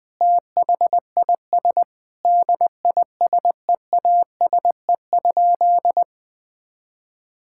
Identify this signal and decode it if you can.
Morse code, heard as a keyed tone, about 20 wpm, 700 Hz. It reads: THIS DISEASE?
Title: Sentences from Top 1000 Words - Review Part 6 - 20wpm